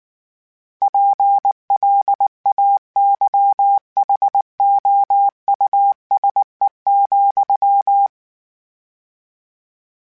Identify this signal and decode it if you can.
Morse code, heard as a keyed tone, about 19 wpm, 800 Hz. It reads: PLAYHOUSE,